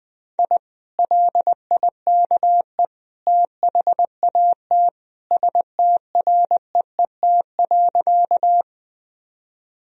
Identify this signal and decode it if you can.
Morse code, heard as a keyed tone, about 21 wpm, 700 Hz. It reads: I LIKE THAT STREET.
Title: Sentences from Top 1000 Words - Review Part 3 - 20wpm